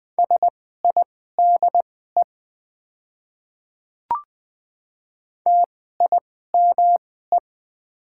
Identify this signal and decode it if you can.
Morse code, heard as a keyed tone, about 20 wpm, spaced as for 14 wpm, 700 Hz. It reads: SIDE TIME
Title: 2x Character Spacing - Top 200 Words - 20wpm